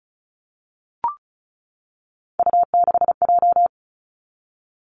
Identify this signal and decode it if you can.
Morse code, heard as a keyed tone, about 36 wpm, 700 Hz. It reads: U6J
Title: Three Challenging Letter-Number Combinations 35wpm